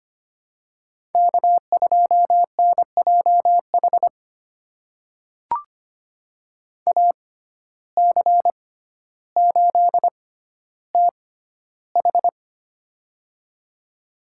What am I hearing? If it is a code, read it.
Morse code, heard as a keyed tone, about 25 wpm, spaced as for 13 wpm, 700 Hz. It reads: K2NJH AC8TH
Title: USA Call Signs - 5 Characters - 6x Character Spacing - 25wpm